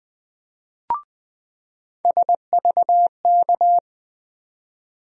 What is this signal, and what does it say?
Morse code, heard as a keyed tone, about 20 wpm, 700 Hz. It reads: SVK
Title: USA Call Sign Postfixes - 3 Letters 20wpm